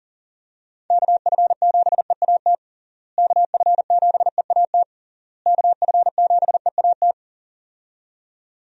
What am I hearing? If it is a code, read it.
Morse code, heard as a keyed tone, about 40 wpm, 700 Hz. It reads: KF7EAT KF7EAT KF7EAT
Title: USA Call Signs - 6 Characters - Repeated 3x - 40wpm